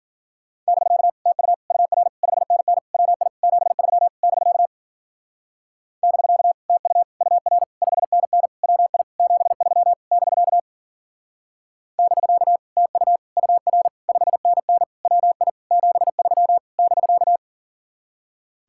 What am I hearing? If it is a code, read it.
Morse code, heard as a keyed tone, about 51 wpm, 700 Hz. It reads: <BK> TU UR 5NN WI 73 <BK> <BK> TU UR 5NN WI 73 <BK> <BK> TU UR 5NN WI 73 <BK>